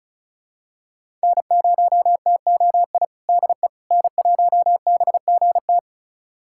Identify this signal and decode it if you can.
Morse code, heard as a keyed tone, about 36 wpm, 700 Hz. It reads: N0TOI DE N1BGT
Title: USA Callsign DE Callsign - 5 Characters - 35wpm